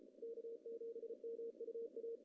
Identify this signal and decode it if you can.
Morse code, heard as a keyed tone, about 34 wpm, 470 Hz. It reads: K7MUA
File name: CW / cqww-cw-k5tr